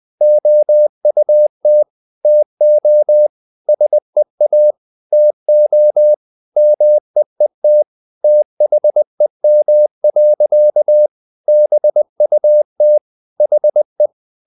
Morse code, heard as a keyed tone, about 20 wpm, 600 Hz. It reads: OUT TO SEA TO MEET THEM. BUT HE
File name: the old man and the sea / the old man and the sea part 1 20wpm